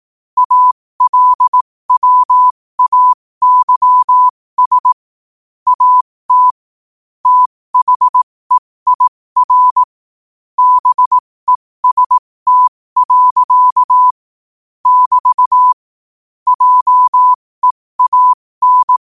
Morse code, heard as a keyed tone, about 18 wpm, spaced as for 15 wpm, 1000 Hz. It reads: ALWAYS AT THEIR BEST. <BT> JEAN
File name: qotdfiles / 15wpm